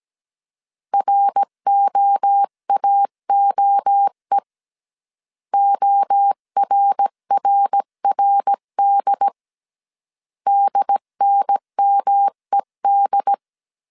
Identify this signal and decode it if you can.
Morse code, heard as a keyed tone, about 17 wpm, 800 Hz. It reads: ROAOE ORRRD DNMED
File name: CW / DET / 1 3 1500C S A 16wpm 2141